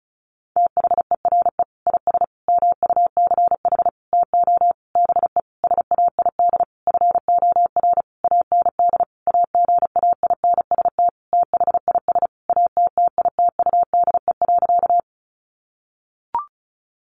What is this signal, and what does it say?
Morse code, heard as a keyed tone, about 34 wpm, 700 Hz. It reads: THERE IS MUCH TO BE SAID FOR AND AGAINST THIS ATTITUDE.
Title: Sentences from Top 1300 Words 35wpm